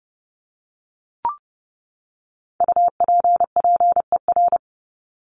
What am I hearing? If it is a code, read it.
Morse code, heard as a keyed tone, about 30 wpm, 700 Hz. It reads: UPPER
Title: Top 1751-2000 Words 30wpm